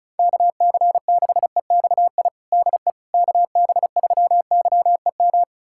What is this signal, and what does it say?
Morse code, heard as a keyed tone, about 36 wpm, 700 Hz. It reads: KC6EXI DE KB3YEM
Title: USA Callsign DE Callsign 35wpm